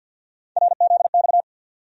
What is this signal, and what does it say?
Morse code, heard as a keyed tone, about 46 wpm, 700 Hz. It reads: RZX
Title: USA Call Sign Postfixes - 3 Letters 50wpm